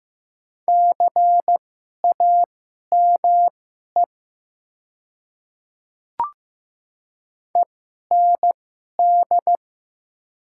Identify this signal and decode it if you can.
Morse code, heard as a keyed tone, about 15 wpm, spaced as for 11 wpm, 700 Hz. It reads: CAME END
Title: 2x Character Spacing - Top 200 Words - 15wpm